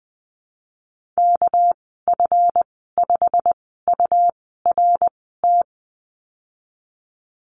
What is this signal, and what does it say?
Morse code, heard as a keyed tone, about 20 wpm, spaced as for 15 wpm, 700 Hz. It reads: KF5URT
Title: 213 Lesson 34g - Number 8 - Call Signs 20wpm